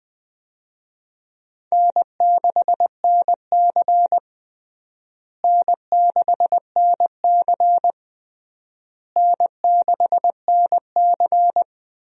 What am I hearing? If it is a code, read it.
Morse code, heard as a keyed tone, about 20 wpm, 700 Hz. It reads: N6NC N6NC N6NC
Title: USA Call Signs - 4 Characters - Repeated 3x - 20wpm